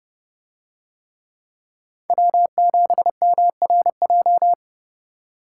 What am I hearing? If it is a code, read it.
Morse code, heard as a keyed tone, about 31 wpm, 700 Hz. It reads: W7MRJ